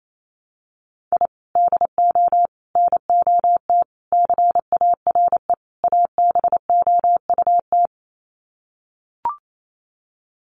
Morse code, heard as a keyed tone, about 28 wpm, 700 Hz. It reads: I DO NOT CARE ABOUT
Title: Sets of 5 Words Encore2 28wpm